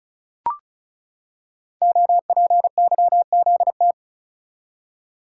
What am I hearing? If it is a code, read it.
Morse code, heard as a keyed tone, about 36 wpm, 700 Hz. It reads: OPYZT
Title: Code Groups - 5 Letters - 35wpm